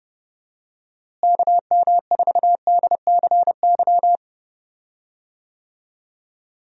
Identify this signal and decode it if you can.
Morse code, heard as a keyed tone, about 31 wpm, 700 Hz. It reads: KM4DCY
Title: USA Call Signs - 6 Characters - Repeated 2x - 30wpm